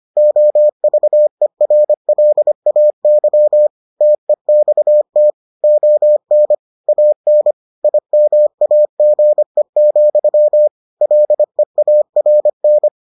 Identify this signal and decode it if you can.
Morse code, heard as a keyed tone, about 25 wpm, 600 Hz. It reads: OVERLAY TEXT ON AN IMAGE, LEARN